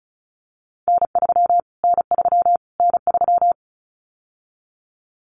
Morse code, heard as a keyed tone, about 34 wpm, 700 Hz. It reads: N3 N3 N3
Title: Callsign Prefixes for Top 25 DXCC - Repeated 3x - 35wpm